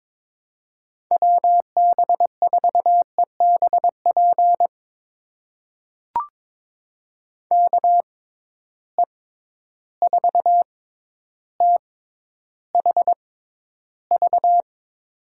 Morse code, heard as a keyed tone, about 22 wpm, spaced as for 11 wpm, 700 Hz. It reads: WB4EBP KE4THV
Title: USA Call Signs - 6 Characters - 6x Character Spacing - 22wpm